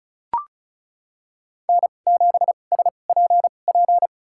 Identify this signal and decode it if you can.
Morse code, heard as a keyed tone, about 36 wpm, spaced as for 25 wpm, 700 Hz. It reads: N7SPP
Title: USA Call Signs - 5 Characters - 2x Character Spacing - 35wpm